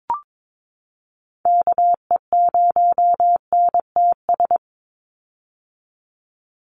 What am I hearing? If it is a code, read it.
Morse code, heard as a keyed tone, about 22 wpm, 700 Hz. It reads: KE0NTS
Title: USA Call Signs 22wpm